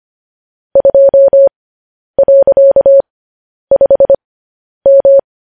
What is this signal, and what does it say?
Morse code, heard as a keyed tone, about 25 wpm, spaced as for 10 wpm, 550 Hz. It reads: 2.5M